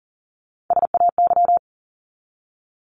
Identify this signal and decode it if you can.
Morse code, heard as a keyed tone, about 40 wpm, 700 Hz. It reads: SAY
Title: Top 200 Words 40wpm